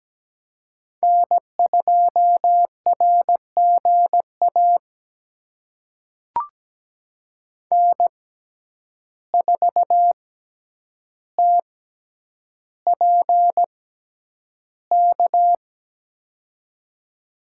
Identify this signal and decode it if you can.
Morse code, heard as a keyed tone, about 17 wpm, spaced as for 9 wpm, 700 Hz. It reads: N2RGA N4TPK
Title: USA Call Signs - 5 Characters - 6x Character Spacing - 17wpm